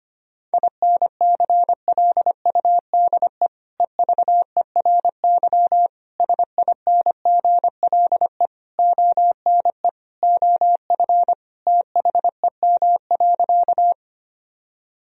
Morse code, heard as a keyed tone, about 25 wpm, 700 Hz. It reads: INCLUDE EVERY SINGLE ONE OF THEM.
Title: Sentences from Top 1000 Words - Review Part 4 - 25wpm